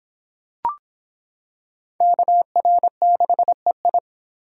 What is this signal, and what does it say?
Morse code, heard as a keyed tone, about 26 wpm, 700 Hz. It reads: KR6EI